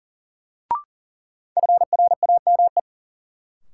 Morse code, heard as a keyed tone, about 40 wpm, 700 Hz. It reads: FRAME
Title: Sentences from Top 1000 Words - Review Part 6 - New Word Drill - 40wpm